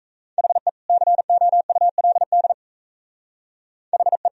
Morse code, heard as a keyed tone, about 35 wpm, 700 Hz. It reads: HE COULD HE